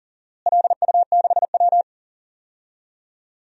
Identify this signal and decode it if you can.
Morse code, heard as a keyed tone, about 37 wpm, 700 Hz. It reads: LU6W
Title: DXCC-100 Call Signs - 4 Characters - Repeated 2x - 40wpm